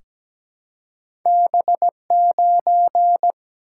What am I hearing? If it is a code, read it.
Morse code, heard as a keyed tone, about 17 wpm, 700 Hz. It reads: B9
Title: Single-Letter Single-Number 17wpm